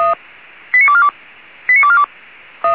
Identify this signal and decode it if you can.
Five-tone selective call (ZVEI), heard as quick runs of five tones, tones of about 70 ms. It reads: 89242 89242